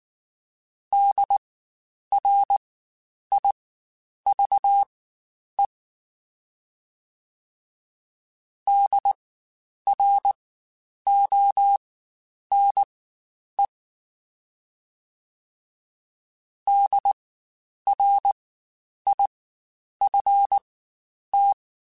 Morse code, heard as a keyed tone, about 19 wpm, spaced as for 9 wpm, 800 Hz. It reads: DRIVE DRONE DRIFT
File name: Spaced Alphas / Spaced Alphas mp3s / Ds sp @18